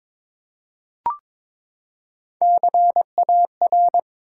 Morse code, heard as a keyed tone, about 22 wpm, 700 Hz. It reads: CAR
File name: practice / 22wpm / Three Letter-Number Combinations - 22wpm